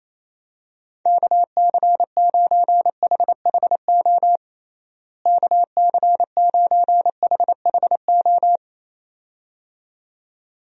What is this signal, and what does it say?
Morse code, heard as a keyed tone, about 28 wpm, 700 Hz. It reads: KC9HHO KC9HHO
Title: USA Call Signs - 6 Characters - Repeated 2x - 28wpm